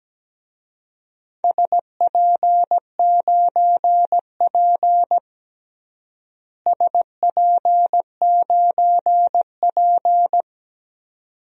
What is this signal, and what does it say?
Morse code, heard as a keyed tone, about 17 wpm, 700 Hz. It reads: SP9P SP9P
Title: DXCC-100 Call Signs - 4 Characters - Repeated 2x - 17wpm